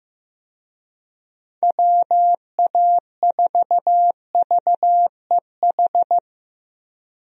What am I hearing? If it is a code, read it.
Morse code, heard as a keyed tone, about 15 wpm, 700 Hz. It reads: WA4VEH
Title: USA Call Signs - 6 Characters - Repeated 2x - 15wpm